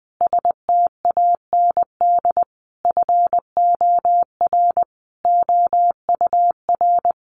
Morse code, heard as a keyed tone, about 20 wpm, 700 Hz. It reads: STAND FOR OUR
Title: Sentences from Top 1500 Words 20wpm